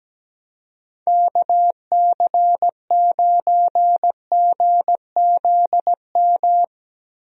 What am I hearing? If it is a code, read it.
Morse code, heard as a keyed tone, about 17 wpm, 700 Hz. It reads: KC9GZM
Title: USA Call Signs - 6 Characters - Repeated 3x - 17wpm